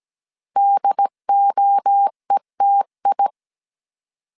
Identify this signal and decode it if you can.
Morse code, heard as a keyed tone, about 17 wpm, 800 Hz. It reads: DOETI
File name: CW / DET / 1 3 1500C S A 16wpm 2141